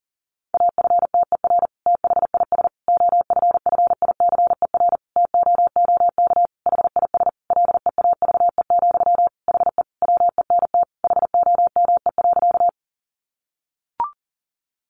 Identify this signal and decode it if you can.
Morse code, heard as a keyed tone, about 40 wpm, 700 Hz. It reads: AFTER THIS OFFICER TOOK HIS LEAVE, HE WENT HOME.
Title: Sentences from Top 1200 Words 40wpm